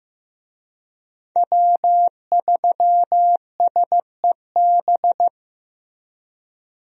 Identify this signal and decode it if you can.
Morse code, heard as a keyed tone, about 15 wpm, 700 Hz. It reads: W3SEB
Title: USA Call Signs - 5 Characters - Repeated 2x - 15wpm